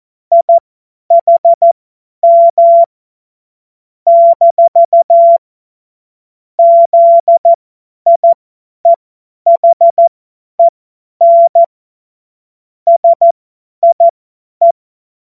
Morse code, heard as a keyed tone, about 14 wpm, spaced as for 10 wpm, 680 Hz. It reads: IHM - ZIEHEN SIE
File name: Morsekurs / Witze / Witz 15. Februar 2022 50 F70